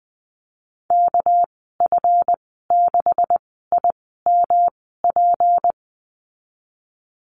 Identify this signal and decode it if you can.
Morse code, heard as a keyed tone, about 20 wpm, spaced as for 14 wpm, 700 Hz. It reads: KF6IMP
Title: 200 Lesson 32g - Letter k - Call Signs 20wpm